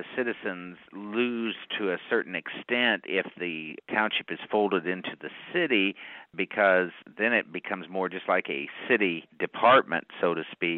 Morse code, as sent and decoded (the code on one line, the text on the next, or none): none